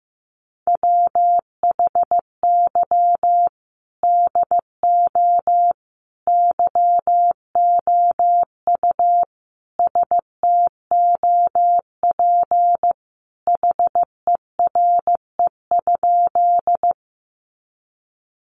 Morse code, heard as a keyed tone, about 14 wpm, 700 Hz. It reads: WHY DO YOU STOP HERE?
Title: Sentences from Top 300 Words 15wpm